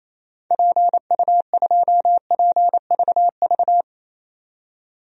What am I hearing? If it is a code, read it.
Morse code, heard as a keyed tone, about 28 wpm, 700 Hz. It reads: PU2PVV
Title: DXCC-100 Call Signs - 6 Characters - 28wpm